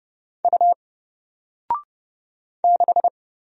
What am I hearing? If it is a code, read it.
Morse code, heard as a keyed tone, about 30 wpm, 700 Hz. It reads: U 6
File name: Single-Letter-Number-30wpm